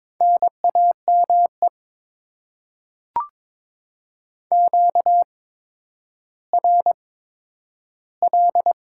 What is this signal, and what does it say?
Morse code, heard as a keyed tone, about 22 wpm, spaced as for 11 wpm, 700 Hz. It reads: NAME QRL
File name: Top 20 QSO Elements - 8x Character Spacing - 22wpm